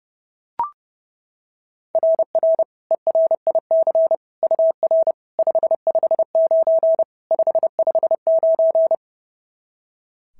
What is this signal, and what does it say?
Morse code, heard as a keyed tone, about 32 wpm, 690 Hz. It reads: RR ERIC UR 559 559
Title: SOTA QSO Practice 30wpm